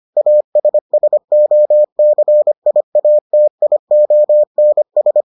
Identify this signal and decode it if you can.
Morse code, heard as a keyed tone, about 25 wpm, 600 Hz. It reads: ASSOCIATIONS